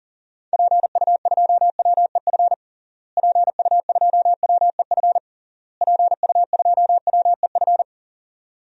Morse code, heard as a keyed tone, about 37 wpm, 700 Hz. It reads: PU2WEF PU2WEF PU2WEF
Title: DXCC-100 Call Signs - 6 Characters - Repeated 3x - 40wpm